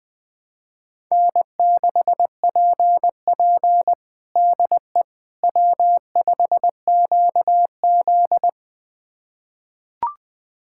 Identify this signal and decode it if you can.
Morse code, heard as a keyed tone, about 21 wpm, 700 Hz. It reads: N6PP DE W5QZ
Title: USA Callsign DE Callsign - 4 Characters - 20wpm